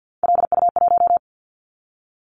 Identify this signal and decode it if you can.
Morse code, heard as a keyed tone, about 50 wpm, 700 Hz. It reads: LU1
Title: Callsign Prefixes for Top 100 DXCC - 50wpm